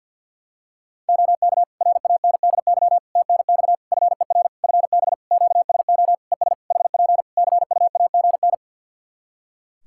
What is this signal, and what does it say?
Morse code, heard as a keyed tone, about 48 wpm, 700 Hz. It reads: OK RANDY TNX FER FB QSO ES HP CUAGN
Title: Practice QSO - Exchange 4 - Ending - 50wpm